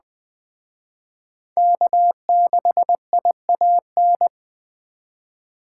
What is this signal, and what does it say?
Morse code, heard as a keyed tone, about 21 wpm, 700 Hz. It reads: K6IAN
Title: Speed Racing - USA 1x3 Call Signs 17wpm